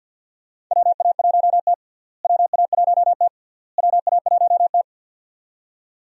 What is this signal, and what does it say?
Morse code, heard as a keyed tone, about 48 wpm, 700 Hz. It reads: WA1T WA1T WA1T
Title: USA Call Signs - 4 Characters - Repeated 3x - 50wpm